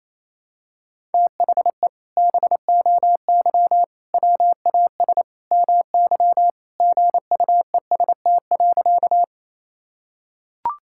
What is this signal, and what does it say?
Morse code, heard as a keyed tone, about 28 wpm, 700 Hz. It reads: THE BOY WAS MY GUEST.
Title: Sentences from Top 1600 Words 28wpm